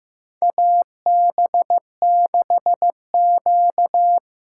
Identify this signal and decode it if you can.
Morse code, heard as a keyed tone, about 15 wpm, 700 Hz. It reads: AB6Q
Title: USA Call Signs - 4 Characters - 4x Character Spacing - 15wpm